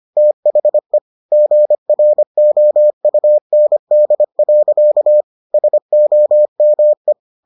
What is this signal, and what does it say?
Morse code, heard as a keyed tone, about 25 wpm, 600 Hz. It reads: THE GROUND. SOME